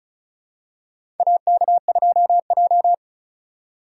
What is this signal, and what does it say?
Morse code, heard as a keyed tone, about 36 wpm, 700 Hz. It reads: AK2J
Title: USA Call Signs - 3x Character Spacing - 35wpm